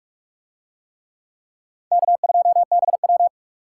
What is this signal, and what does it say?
Morse code, heard as a keyed tone, about 43 wpm, 700 Hz. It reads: K2BW